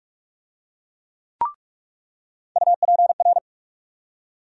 Morse code, heard as a keyed tone, about 46 wpm, 700 Hz. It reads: UPR
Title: USA Call Sign Postfixes - 3 Letters 45wpm